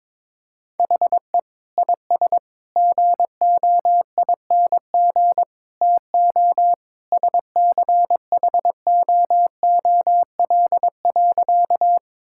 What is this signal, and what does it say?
Morse code, heard as a keyed tone, about 21 wpm, 700 Hz. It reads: HE IS GOING TO SCHOOL.
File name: practice / 22wpm / Sentences with 3-Word Phrases - 22wpm